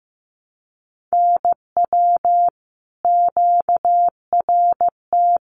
Morse code, heard as a keyed tone, about 15 wpm, 700 Hz. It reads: NW QRT
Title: Common QSO Phrases 15wpm